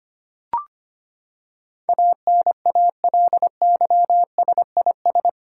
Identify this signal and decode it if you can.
Morse code, heard as a keyed tone, about 25 wpm, 700 Hz. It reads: ANALYSIS
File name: Eight Letter Words - 25wpm